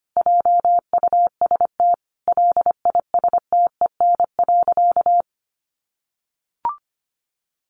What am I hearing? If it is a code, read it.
Morse code, heard as a keyed tone, about 25 wpm, 700 Hz. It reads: JUST LISTEN.
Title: Sentences from Top 1300 Words 25wpm